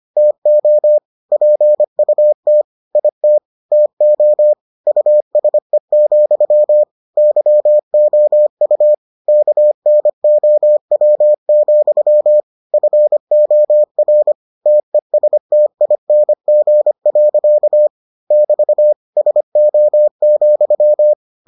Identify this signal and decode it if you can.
Morse code, heard as a keyed tone, about 25 wpm, 600 Hz. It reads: TO PUT IT TO USE, YOU KNOW, FOR TESTING. <BT> SO,